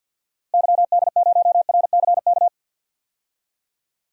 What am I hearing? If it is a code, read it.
Morse code, heard as a keyed tone, about 53 wpm, 700 Hz. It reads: YD0RCK
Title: DXCC-100 Call Signs - 6 Characters - Repeated 3x - 50wpm